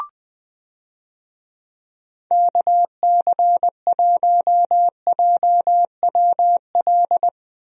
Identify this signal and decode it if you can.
Morse code, heard as a keyed tone, about 21 wpm, 700 Hz. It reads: KC1JWL
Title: USA Call Signs - 6 Characters - Repeated 2x - 20wpm